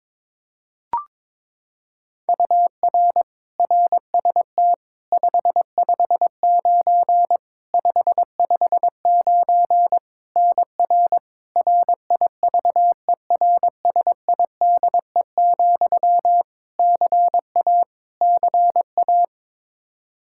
Morse code, heard as a keyed tone, about 22 wpm, 700 Hz. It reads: UR RST 559 559 NR RIVERSIDE, CA CA
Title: Practice QSO - Exchange 1 - Beginning - 22wpm